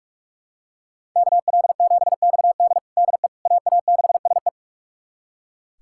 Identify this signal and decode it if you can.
Morse code, heard as a keyed tone, about 42 wpm, 700 Hz. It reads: KL7XD DE AA6SE
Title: USA Callsign DE Callsign - 5 Characters - 45wpm